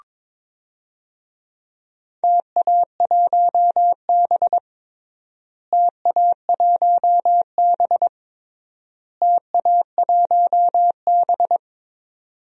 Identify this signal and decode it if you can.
Morse code, heard as a keyed tone, about 22 wpm, 700 Hz. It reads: TA1B TA1B TA1B
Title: DXCC-100 Call Signs - 4 Characters - Repeated 3x - 22wpm